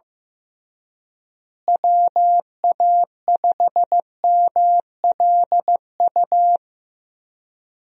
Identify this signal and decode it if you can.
Morse code, heard as a keyed tone, about 14 wpm, 700 Hz. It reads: WA5MLU
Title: USA Call Signs - 6 Characters - Repeated 3x - 15wpm